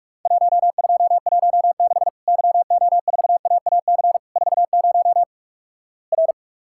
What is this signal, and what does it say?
Morse code, heard as a keyed tone, about 46 wpm, 700 Hz. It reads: 1216 YO4AAC 40 R
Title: Worked All Europe DX Contest - QTC Exchange - 45wpm